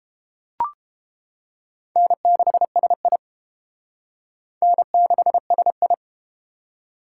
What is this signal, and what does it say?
Morse code, heard as a keyed tone, about 30 wpm, 700 Hz. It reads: N6SI N6SI